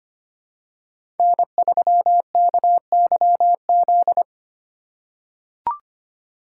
Morse code, heard as a keyed tone, about 25 wpm, 700 Hz. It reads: N3KYZ
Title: USA Call Signs - 5 Characters - 2x Character Spacing - 25wpm